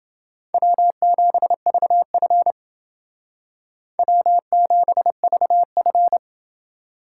Morse code, heard as a keyed tone, about 28 wpm, 700 Hz. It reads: W7VF W7VF